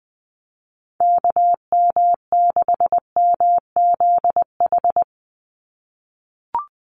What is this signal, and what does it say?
Morse code, heard as a keyed tone, about 20 wpm, 700 Hz. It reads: KM6MZH
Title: USA Call Signs 20wpm